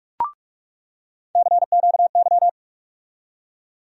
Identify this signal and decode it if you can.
Morse code, heard as a keyed tone, about 46 wpm, 700 Hz. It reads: CQY
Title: USA Call Sign Postfixes - 3 Letters 45wpm